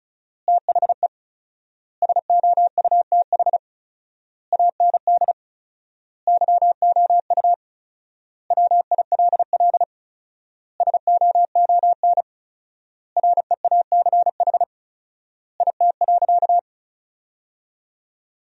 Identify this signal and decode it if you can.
Morse code, heard as a keyed tone, about 36 wpm, 700 Hz. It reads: THE SOUTH AND YOU WILL SOON REACH IT.
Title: 4x Word-Spacing - Sentences from Top 400 Words - 35wpm